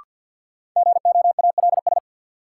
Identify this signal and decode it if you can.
Morse code, heard as a keyed tone, about 47 wpm, 700 Hz. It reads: GOALS
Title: Sentences from Top 1000 Words - Review Part 6 - New Word Drill - 50wpm